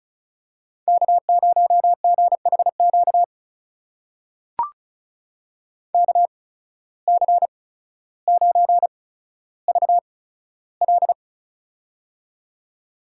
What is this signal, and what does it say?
Morse code, heard as a keyed tone, about 36 wpm, spaced as for 15 wpm, 700 Hz. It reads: K0GHQ KC9VL
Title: USA Call Signs - 5 Characters - 8x Character Spacing - 35wpm